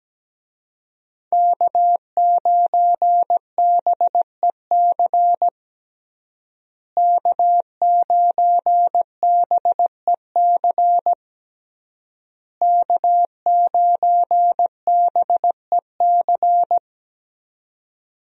Morse code, heard as a keyed tone, about 17 wpm, 700 Hz. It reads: K9BEC K9BEC K9BEC
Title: USA Call Signs - 5 Characters - Repeated 3x - 17wpm